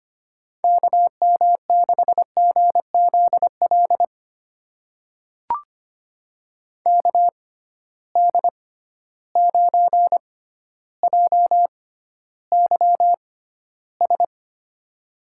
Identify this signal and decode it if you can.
Morse code, heard as a keyed tone, about 25 wpm, spaced as for 13 wpm, 700 Hz. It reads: KM6GZL KD9JYS
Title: USA Call Signs - 6 Characters - 6x Character Spacing - 25wpm